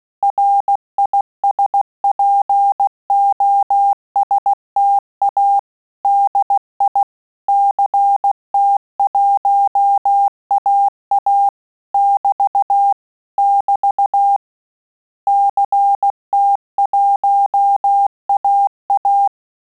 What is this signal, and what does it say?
Morse code, heard as a keyed tone, about 16 wpm, 800 Hz. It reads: RISPOSTA DI CT1AA <BT> <BT> CT1AA
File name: Telegraphy-group / QSO CW 15 PPM